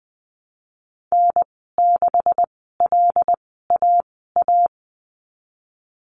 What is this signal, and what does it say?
Morse code, heard as a keyed tone, about 20 wpm, spaced as for 15 wpm, 700 Hz. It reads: N6LAA
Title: 173 Lesson 28g - Number 9 - Call Signs 20wpm